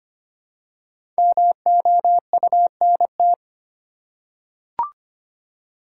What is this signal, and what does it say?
Morse code, heard as a keyed tone, about 25 wpm, 700 Hz. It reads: MOUNT
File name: 5 Letter Words - First 3 Letters Given - 25wpm